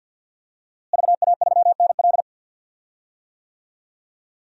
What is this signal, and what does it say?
Morse code, heard as a keyed tone, about 52 wpm, 700 Hz. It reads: VA3NL
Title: DXCC-100 Call Signs - 5 Characters - Repeated 3x - 50wpm